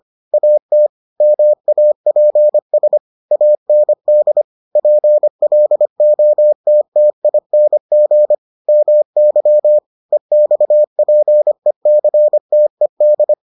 Morse code, heard as a keyed tone, about 25 wpm, 600 Hz. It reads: AT MAPS AND PLOTTING MY EXPECTED